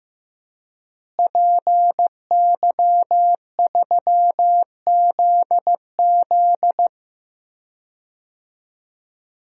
Morse code, heard as a keyed tone, about 15 wpm, 700 Hz. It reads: PY3ZZ
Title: DXCC-100 Call Signs - 5 Characters - Repeated 2x - 15wpm